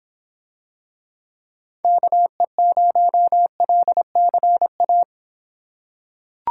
Morse code, heard as a keyed tone, about 25 wpm, 700 Hz. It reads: KE0LCA